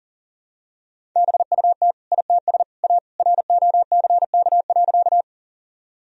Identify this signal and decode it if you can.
Morse code, heard as a keyed tone, about 40 wpm, 700 Hz. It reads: BUT ITS A ROCK.
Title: Sentences from Top 1000 Words - Review Part 2 - 40wpm